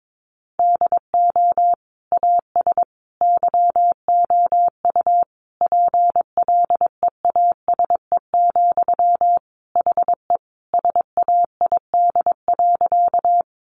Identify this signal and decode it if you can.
Morse code, heard as a keyed tone, about 22 wpm, 700 Hz. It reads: DO AS YOU PLEASE, HE SAID.
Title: Sentences from Top 1500 Words 22wpm